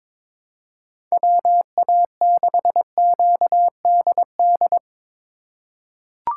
Morse code, heard as a keyed tone, about 22 wpm, 700 Hz. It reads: WA6QDD
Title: USA Call Signs - 6 Characters - 4x Character Spacing - 22wpm